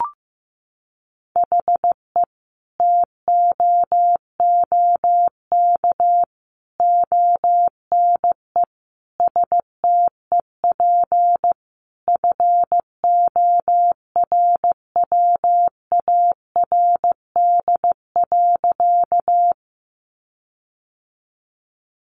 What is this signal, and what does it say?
Morse code, heard as a keyed tone, about 15 wpm, 700 Hz. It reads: HE TOOK ONE STEP FORWARD.
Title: Sentences from Top 1300 Words 15wpm